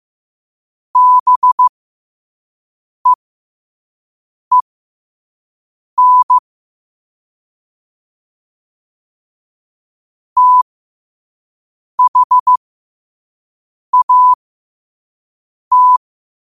Morse code, heard as a keyed tone, about 15 wpm, spaced as for 5 wpm, 1000 Hz. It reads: BEEN THAT
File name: qotdfiles / 5wpm